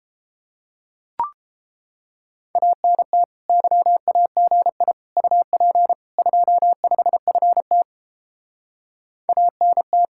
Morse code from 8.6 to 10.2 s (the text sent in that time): ANT